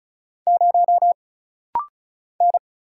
Morse code, heard as a keyed tone, about 34 wpm, 700 Hz. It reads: 0 N
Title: Single Letter-Number Alt - Rapid-Fire 35wpm